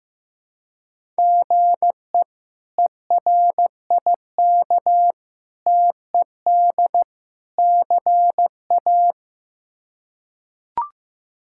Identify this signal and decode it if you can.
Morse code, heard as a keyed tone, about 15 wpm, 700 Hz. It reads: GE ERIK TED CA
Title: K1USN SST QSO Practice - 15wpm